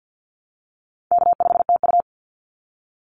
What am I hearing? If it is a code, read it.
Morse code, heard as a keyed tone, about 50 wpm, 700 Hz. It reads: K5TU
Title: Call Signs Repeated 3x - 50wpm